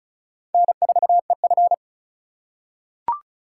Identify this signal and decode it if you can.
Morse code, heard as a keyed tone, about 34 wpm, 690 Hz. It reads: N4EF
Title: USA Call Signs - 4x Character Spacing - 35wpm